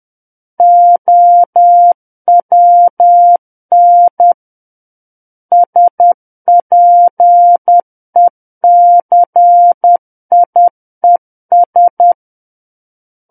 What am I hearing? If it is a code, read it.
Morse code, heard as a keyed tone, about 10 wpm, 700 Hz. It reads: OWN SPECIES
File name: Book One / 01-Chapter 1 - 10 WPM